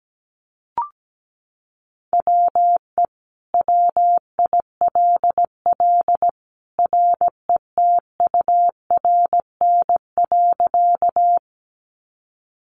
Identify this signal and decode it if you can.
Morse code, heard as a keyed tone, about 17 wpm, 700 Hz. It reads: WE WILL RETURN.
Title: Sentences from Top 800 Words 17wpm